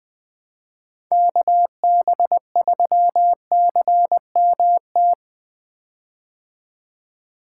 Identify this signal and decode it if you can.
Morse code, heard as a keyed tone, about 20 wpm, 700 Hz. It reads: KB3CMT
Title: 2023 POTA Activator Call Signs - 20wpm